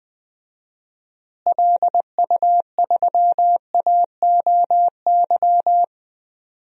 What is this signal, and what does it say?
Morse code, heard as a keyed tone, about 20 wpm, 700 Hz. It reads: LU3AOY
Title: DXCC-100 Call Signs - 6 Characters - Repeated 3x - 20wpm